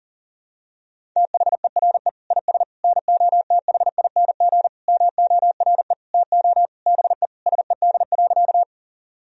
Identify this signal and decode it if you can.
Morse code, heard as a keyed tone, about 40 wpm, 700 Hz. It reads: THERE IS NOTHING MORE TO BE SED.
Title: Sentences from Top 500 Words and Abbreviations 40wpm